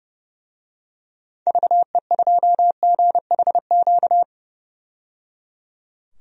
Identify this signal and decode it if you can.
Morse code, heard as a keyed tone, about 30 wpm, 700 Hz. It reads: VE2GHQ
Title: DXCC-100 Call Signs - 6 Characters - 30wpm